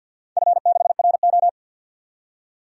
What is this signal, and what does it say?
Morse code, heard as a keyed tone, about 51 wpm, 700 Hz. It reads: F6RO
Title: DXCC-100 Call Signs - 4 Characters - 50wpm